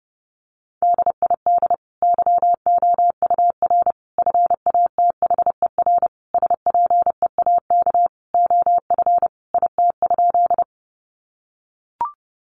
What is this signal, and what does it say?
Morse code, heard as a keyed tone, about 30 wpm, 700 Hz. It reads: DID YOUR FATHER SPEAK OF IT?
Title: Sentences from Top 1100 Words 30wpm